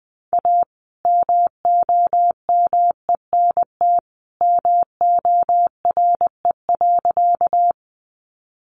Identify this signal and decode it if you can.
Morse code, heard as a keyed tone, about 20 wpm, 700 Hz. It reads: A MOMENT MORE.
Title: Sentences from Top 1000 Words 20wpm